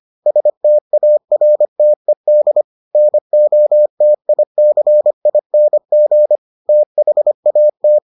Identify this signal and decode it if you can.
Morse code, heard as a keyed tone, about 25 wpm, 600 Hz. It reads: STARTED NOTICING THAT